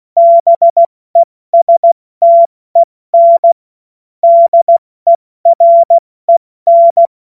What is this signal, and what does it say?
Morse code, heard as a keyed tone, about 16 wpm, 680 Hz. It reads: BESTEN DEREN